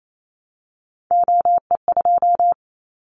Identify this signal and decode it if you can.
Morse code, heard as a keyed tone, about 28 wpm, 700 Hz. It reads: OE2
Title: Callsign Prefixes for Top 100 DXCC - 28wpm